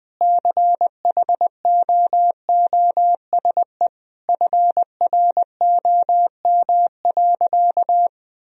Morse code, heard as a keyed tone, about 20 wpm, 700 Hz. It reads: CHOOSE FROM.
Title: Sentences from Top 1000 Words - Review Part 4 - 20wpm